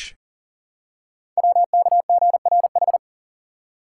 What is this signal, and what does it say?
Morse code, heard as a keyed tone, about 42 wpm, 700 Hz. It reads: WKGRH
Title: Code Groups - 5 Letters - 40wpm